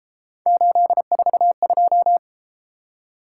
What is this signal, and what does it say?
Morse code, heard as a keyed tone, about 32 wpm, 700 Hz. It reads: 842